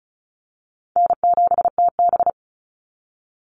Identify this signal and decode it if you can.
Morse code, heard as a keyed tone, about 34 wpm, 700 Hz. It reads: N7TB
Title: Call Signs - 35wpm